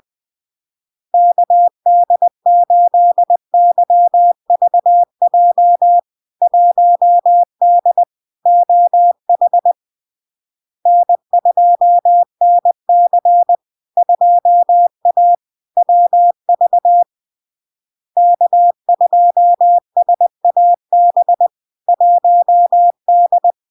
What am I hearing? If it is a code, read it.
Morse code, heard as a keyed tone, about 20 wpm, 700 Hz. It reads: KD8YVJ 1D OH N2NC 2A WV K2SAB 1D
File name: LARK-FD-2022-Log-in-CW-20wpm-700-Hz